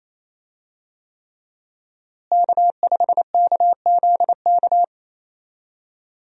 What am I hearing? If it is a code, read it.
Morse code, heard as a keyed tone, about 28 wpm, 700 Hz. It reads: K5KZK
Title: USA Call Signs - 5 Characters - Repeated 3x - 28wpm